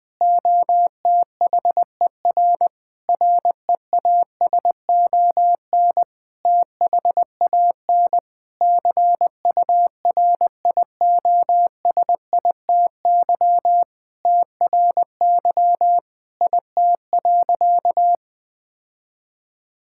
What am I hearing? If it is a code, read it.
Morse code, heard as a keyed tone, about 20 wpm, 700 Hz. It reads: OTHER REASON THAN CURIOSITY TRY IT.
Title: Sentences with 6-Word Phrases - Spoken Beginning - 20wpm